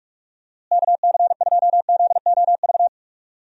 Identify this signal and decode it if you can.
Morse code, heard as a keyed tone, about 46 wpm, 700 Hz. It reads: KC2ZOV
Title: USA Call Signs - 6 Characters - Repeated 2x - 45wpm